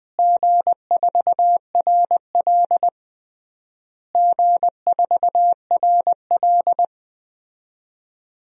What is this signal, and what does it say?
Morse code, heard as a keyed tone, about 21 wpm, 700 Hz. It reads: G4RL G4RL
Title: DXCC-100 Call Signs - 4 Characters - Repeated 3x - 20wpm